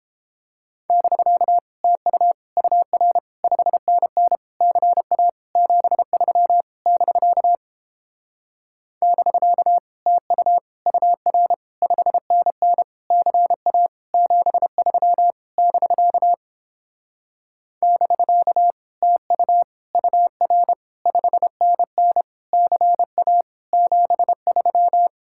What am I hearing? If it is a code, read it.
Morse code, heard as a keyed tone, about 29 wpm, 700 Hz. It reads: <BK> TU UR 5NN CA 73 <BK> <BK> TU UR 5NN CA 73 <BK> <BK> TU UR 5NN CA 73